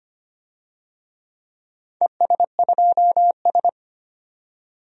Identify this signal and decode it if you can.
Morse code, heard as a keyed tone, about 25 wpm, 700 Hz. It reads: ES2S
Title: DXCC-100 Call Signs - 4 Characters - 25wpm